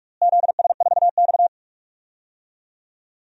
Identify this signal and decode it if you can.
Morse code, heard as a keyed tone, about 44 wpm, 700 Hz. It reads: ZS4X
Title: DXCC-100 Call Signs - 4 Characters - Repeated 2x - 45wpm